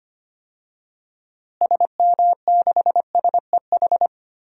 Morse code, heard as a keyed tone, about 25 wpm, 700 Hz. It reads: SM6SEH
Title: DXCC-100 Call Signs - 6 Characters - Repeated 2x - 25wpm